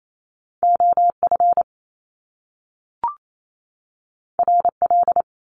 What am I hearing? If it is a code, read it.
Morse code, heard as a keyed tone, about 28 wpm, 700 Hz. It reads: OF RL
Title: Top Two Letter Combinations in Words 28wpm